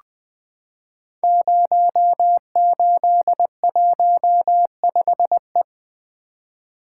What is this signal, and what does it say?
Morse code, heard as a keyed tone, about 20 wpm, 700 Hz. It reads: 0815E
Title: Code Groups - 5 Letter:Numbers - 20wpm